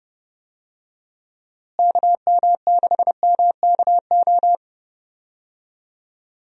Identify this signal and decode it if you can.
Morse code, heard as a keyed tone, about 32 wpm, 700 Hz. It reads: KM6MKO
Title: USA Call Signs - 6 Characters - Repeated 3x - 30wpm